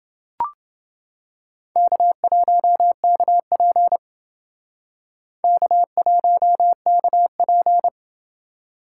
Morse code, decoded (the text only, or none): K1KP K1KP